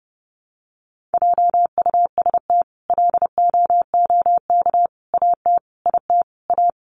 Morse code, heard as a keyed tone, about 30 wpm, 700 Hz. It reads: JUST LOOK AT IT A
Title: Sentences from Top 1000 Words 30wpm